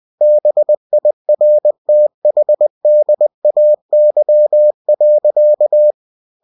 Morse code, heard as a keyed tone, about 20 wpm, 600 Hz. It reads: BIRTHDAY.
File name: cw-books / winnie-the-pooh / winnie-20wpm-page-0047